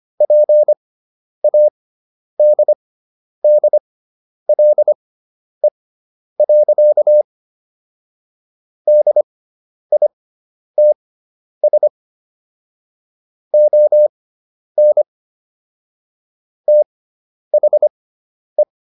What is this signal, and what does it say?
Morse code, heard as a keyed tone, about 25 wpm, spaced as for 10 wpm, 600 Hz. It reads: PADDLE. DITS ON THE